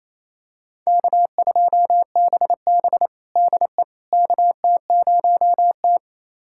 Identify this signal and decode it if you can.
Morse code, heard as a keyed tone, about 28 wpm, 700 Hz. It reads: K2BB DE KT0T
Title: USA Callsign DE Callsign - 4 Characters - 28wpm